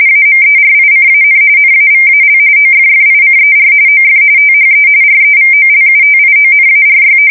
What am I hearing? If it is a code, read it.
RTTY, about 45.45 baud. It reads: AMERICAN NAVAL OFFICER OF THE REVOLUTIONARY
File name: fsk170 / usnavy1